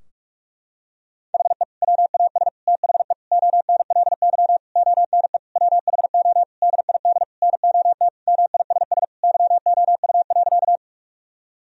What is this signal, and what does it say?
Morse code, heard as a keyed tone, about 46 wpm, 700 Hz. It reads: HE WAS THE ONLY ONE WHO DID NOT MISS YOU.
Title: Sentences from Top 1000 Words - Review Part 3 - 45wpm